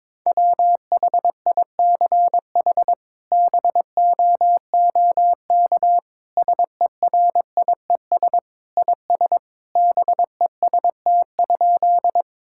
Morse code, heard as a keyed tone, about 22 wpm, 700 Hz. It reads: WHICH BOOK SERIES IS BEST?
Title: Sentences from Top 1000 Words - Review Part 5 - 22wpm